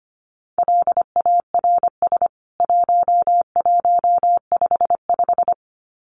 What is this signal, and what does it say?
Morse code, heard as a keyed tone, about 25 wpm, 700 Hz. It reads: LARS 1155
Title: CWT Exchange 25wpm